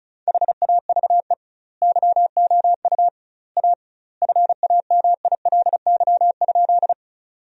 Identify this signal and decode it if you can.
Morse code, heard as a keyed tone, about 34 wpm, 700 Hz. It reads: HAVE YOU A FAMILY?
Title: 2x Word-Spacing - Sentences from Top 400 Words - 35wpm